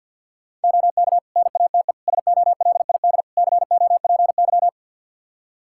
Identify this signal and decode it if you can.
Morse code, heard as a keyed tone, about 52 wpm, 700 Hz. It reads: OK NATE SOLID COPY
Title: Practice QSO - Exchange 3 - More Info - 50wpm